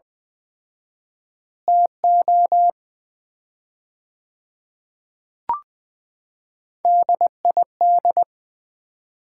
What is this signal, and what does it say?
Morse code, heard as a keyed tone, about 20 wpm, 700 Hz. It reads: TO DID
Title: 4x Word-Spacing - Sets of 4 Words - 20wpm